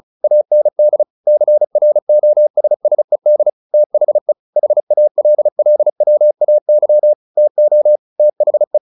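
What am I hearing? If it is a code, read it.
Morse code, heard as a keyed tone, about 34 wpm, 600 Hz. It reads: AND CROSSED THE HALLWAY TO THE
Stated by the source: Morse practice at 35 wpm from CW med Gnister, sending Ken Follett's novel Never in English.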